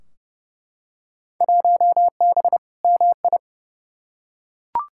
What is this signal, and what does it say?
Morse code, heard as a keyed tone, about 30 wpm, 700 Hz. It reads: 1B MI
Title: Field Day Exchanges - 30wpm